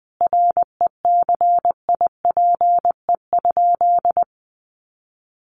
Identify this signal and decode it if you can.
Morse code, heard as a keyed tone, about 20 wpm, 700 Hz. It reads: RECIPE?
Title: Sentences from Top 1200 Words 20wpm